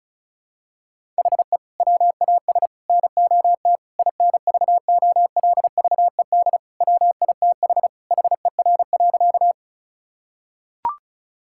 Morse code, heard as a keyed tone, about 36 wpm, 700 Hz. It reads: HE WAS NOT INVOLVED WITH HER.
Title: Sentences from Top 1000 Words - Review Part 5 - 35wpm